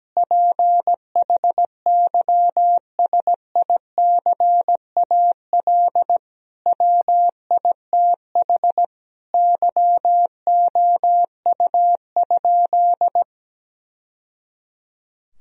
Morse code, heard as a keyed tone, about 17 wpm, 700 Hz. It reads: PHYSICAL WITH YOU?
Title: Sentences from Top 1000 Words - Review Part 4 - 17wpm